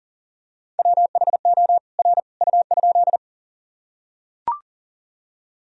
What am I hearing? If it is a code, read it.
Morse code, heard as a keyed tone, about 40 wpm, 700 Hz. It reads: WHO R U?
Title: Sentences from Top 100 Words and Abbreviations 40wpm